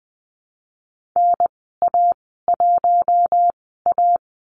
Morse code, heard as a keyed tone, about 20 wpm, spaced as for 15 wpm, 700 Hz. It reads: NA1A
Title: 048 Lesson 9g - Number 4 - Call Signs 20wpm